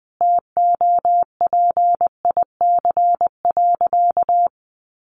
Morse code, heard as a keyed tone, about 20 wpm, 700 Hz. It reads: TOPIC.
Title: Sentences from Top 1200 Words 20wpm